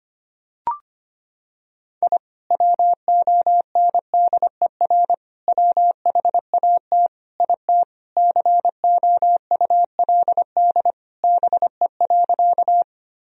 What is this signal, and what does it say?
Morse code, heard as a keyed tone, about 25 wpm, 700 Hz. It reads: I WONDER WHAT IT COULD BE.
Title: Sentences from Top 1000 Words - Review Part 3 - 25wpm